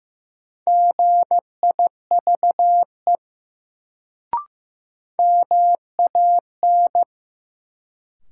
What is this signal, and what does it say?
Morse code, heard as a keyed tone, about 15 wpm, 700 Hz. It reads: GIVE MAN
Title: Sentences from Top 200 Words - New Word Drill - 15wpm